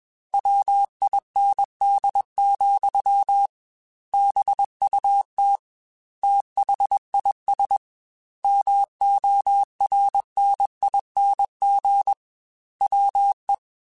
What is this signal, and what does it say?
Morse code, heard as a keyed tone, about 21 wpm, 800 Hz. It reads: WIND, BUT THIS MORNING WE
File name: Weather Sets / Weather Sets-mp3s / Weather 3 17 @20